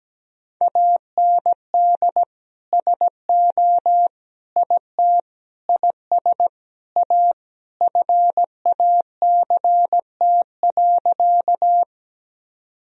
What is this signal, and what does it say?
Morse code, heard as a keyed tone, about 17 wpm, 700 Hz. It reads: AND SO IT IS A FACT.
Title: Sentences from Top 1000 Words - Review Part 3 - 17wpm